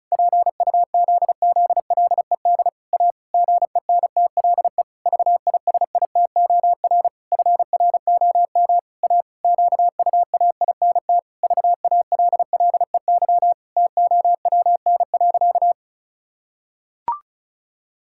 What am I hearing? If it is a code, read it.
Morse code, heard as a keyed tone, about 36 wpm, 700 Hz. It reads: PUZZLED A GENTLE VISITOR FROM A QUAINT VALLEY TOWN.
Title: Pangram Challenge v2 - 35wpm